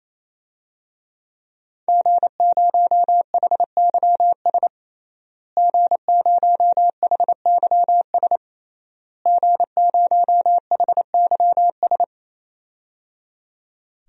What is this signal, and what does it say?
Morse code, heard as a keyed tone, about 27 wpm, 700 Hz. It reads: G0HYS G0HYS G0HYS
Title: DXCC-100 Call Signs - 5 Characters - Repeated 3x - 28wpm